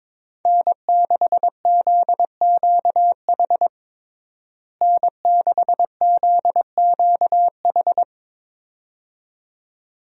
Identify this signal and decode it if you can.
Morse code, heard as a keyed tone, about 22 wpm, 700 Hz. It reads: N6ZQH N6ZQH
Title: USA Call Signs - 5 Characters - Repeated 2x - 22wpm